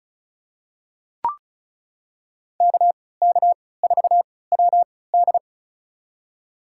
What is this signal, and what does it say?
Morse code, heard as a keyed tone, about 36 wpm, spaced as for 20 wpm, 700 Hz. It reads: KK4WD
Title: USA Call Signs - 5 Characters - 3x Character Spacing - 35wpm